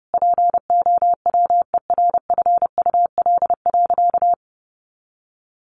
Morse code, heard as a keyed tone, about 30 wpm, 700 Hz. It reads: POWERFUL.
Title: Sentences from Top 1300 Words 30wpm